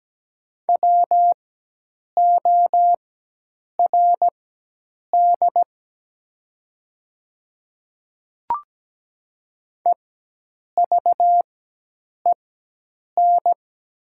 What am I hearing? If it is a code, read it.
Morse code, heard as a keyed tone, about 17 wpm, spaced as for 8 wpm, 700 Hz. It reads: WORD EVEN
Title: 4x Character Spacing - Top 200 Words - 17wpm